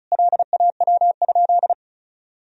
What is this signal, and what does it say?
Morse code, heard as a keyed tone, about 36 wpm, 700 Hz. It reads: LAW?
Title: Sentences from Top 1000 Words - Review Part 4 - 35wpm